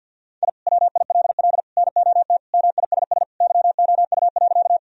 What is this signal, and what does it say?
Morse code, heard as a keyed tone, about 50 wpm, 700 Hz. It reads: I WILL NOT MISS YOU.
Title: Sentences from Top 1000 Words - Review Part 3 - 50wpm